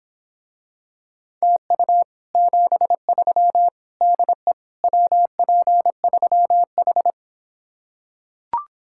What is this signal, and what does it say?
Morse code, heard as a keyed tone, about 26 wpm, 700 Hz. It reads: TU 73 DE WP3H